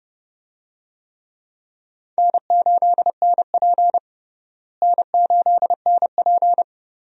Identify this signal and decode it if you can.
Morse code, heard as a keyed tone, about 30 wpm, 700 Hz. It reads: N8NP N8NP
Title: USA Call Signs - 4 Characters - Repeated 2x - 30wpm